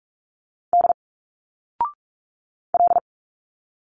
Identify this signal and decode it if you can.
Morse code, heard as a keyed tone, about 46 wpm, 700 Hz. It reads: D L
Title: Single Letters 45wpm